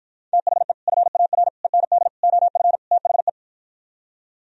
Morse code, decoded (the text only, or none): THE FAR END OF THE